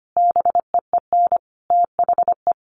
Morse code, heard as a keyed tone, about 25 wpm, 700 Hz. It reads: BEEN THE
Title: Sentences from Top 1200 Words 25wpm